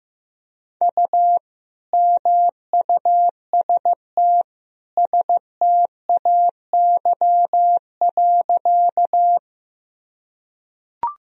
Morse code, heard as a keyed tone, about 15 wpm, 700 Hz. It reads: U MUST STAY.
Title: Sentences from Top 500 Words and Abbreviations 15wpm